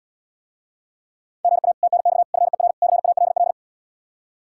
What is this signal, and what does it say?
Morse code, heard as a keyed tone, about 79 wpm, 700 Hz. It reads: GA TNX FER CALL